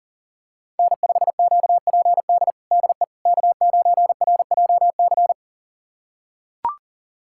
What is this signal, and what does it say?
Morse code, heard as a keyed tone, about 40 wpm, 700 Hz. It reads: N5QPD DE K9RJC
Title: USA Callsign DE Callsign - 5 Characters - 40wpm